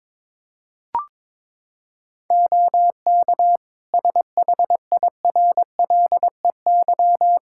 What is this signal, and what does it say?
Morse code, heard as a keyed tone, about 22 wpm, 700 Hz. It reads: OK SHIRLEY
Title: Practice QSO - Exchange 3 - More Info - 22wpm